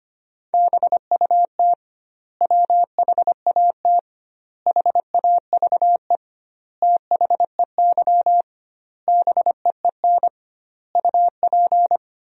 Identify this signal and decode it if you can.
Morse code, heard as a keyed tone, about 25 wpm, 700 Hz. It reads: BUT WHAT HAVE THEY BEEN UP
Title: 2x Word-Spacing - Sentences from Top 100 Words - 25wpm